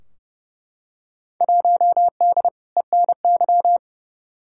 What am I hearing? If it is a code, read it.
Morse code, heard as a keyed tone, about 30 wpm, 700 Hz. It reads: 1D ENY
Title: Field Day Exchanges Encore - 30wpm